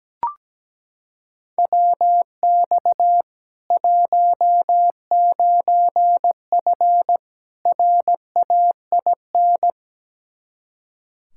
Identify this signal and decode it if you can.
Morse code, heard as a keyed tone, about 17 wpm, 700 Hz. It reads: WX 19F RAIN
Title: K1USN SST QSO Practice 17wpm